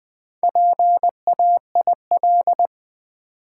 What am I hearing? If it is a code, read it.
Morse code, heard as a keyed tone, about 20 wpm, 700 Hz. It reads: PAIL
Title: Easy Words with Hints - 20wpm